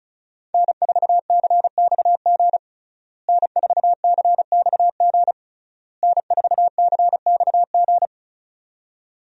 Morse code, heard as a keyed tone, about 36 wpm, 700 Hz. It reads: N4CXG N4CXG N4CXG
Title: USA Call Signs - 5 Characters - Repeated 3x - 35wpm